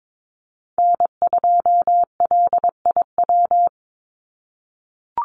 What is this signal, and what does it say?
Morse code, heard as a keyed tone, about 22 wpm, 700 Hz. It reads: N2LIW
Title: USA Call Signs 22wpm